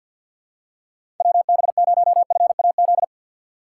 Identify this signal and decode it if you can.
Morse code, heard as a keyed tone, about 51 wpm, 700 Hz. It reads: WB0FAZ
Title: USA Call Signs - 6 Characters - 3x Character Spacing - 50wpm